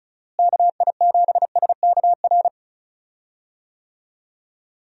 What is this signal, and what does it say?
Morse code, heard as a keyed tone, about 36 wpm, 700 Hz. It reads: KI7SKR
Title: USA Call Signs - 6 Characters - Repeated 2x - 35wpm